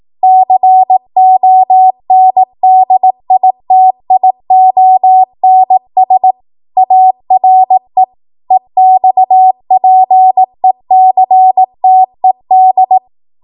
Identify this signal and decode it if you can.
Morse code, heard as a keyed tone, about 18 wpm, 750 Hz. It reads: CONDITIONS ARE EXPECTED
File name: Morse / 18WPM